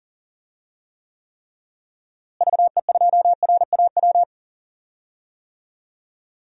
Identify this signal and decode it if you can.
Morse code, heard as a keyed tone, about 40 wpm, 700 Hz. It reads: VE2RAW
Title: DXCC-100 Call Signs - 6 Characters - Repeated 3x - 40wpm